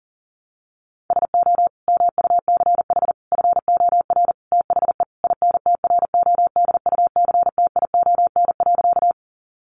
Morse code, heard as a keyed tone, about 40 wpm, 700 Hz. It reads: SO MUCH FOR THE INTRODUCTION.
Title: Sentences from Top 1000 Words 40wpm